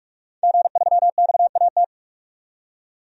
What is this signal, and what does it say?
Morse code, heard as a keyed tone, about 46 wpm, 700 Hz. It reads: G3XAT
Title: DXCC-100 Call Signs - 5 Characters - 45wpm